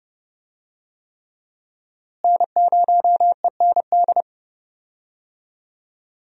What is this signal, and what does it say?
Morse code, heard as a keyed tone, about 30 wpm, 700 Hz. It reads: N0END
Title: USA Call Signs - 5 Characters - Repeated 3x - 30wpm